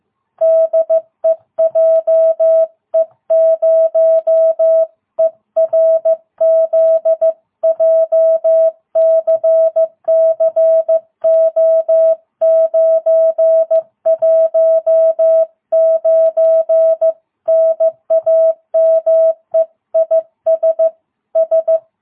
Morse code, heard as a keyed tone, about 14 wpm, 660 Hz. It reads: DE JE0ERZ JCCO919 NAME IS S
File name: jh8cbh / 2022cw